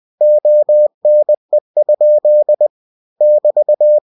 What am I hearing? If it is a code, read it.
Morse code, heard as a keyed tone, about 20 wpm, 600 Hz. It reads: ONE? <BT>
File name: cw-books / winnie-the-pooh / winnie-20wpm-page-0081